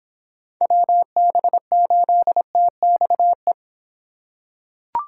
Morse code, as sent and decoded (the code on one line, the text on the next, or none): .-- -... ---.. - -..- .
WB8TXE